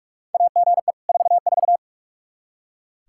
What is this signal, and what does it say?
Morse code, heard as a keyed tone, about 46 wpm, 700 Hz. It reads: AGE 44
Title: Practice QSO - Exchange 3 - More Info - 45wpm